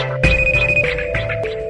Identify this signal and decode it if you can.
There is no code.